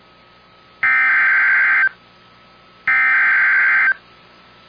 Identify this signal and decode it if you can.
Emergency Alert System (SAME) tones (header bursts), heard as harsh buzzing bursts. It is heard from 0.8 to 3.9 s.